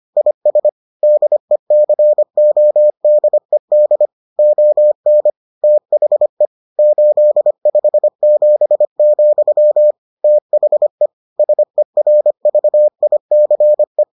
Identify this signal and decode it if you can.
Morse code, heard as a keyed tone, about 25 wpm, 600 Hz. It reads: IS DECODED ON THE 857, THE SERVICE